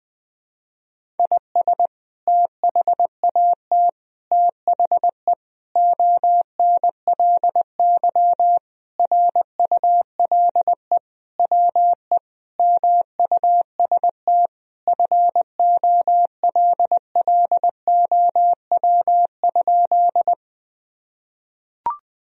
Morse code, heard as a keyed tone, about 21 wpm, 700 Hz. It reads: IS THAT THE ONLY RULE WE MUST FOLLOW?
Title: Sentences from Top 1000 Words - Review Part 2 - 20wpm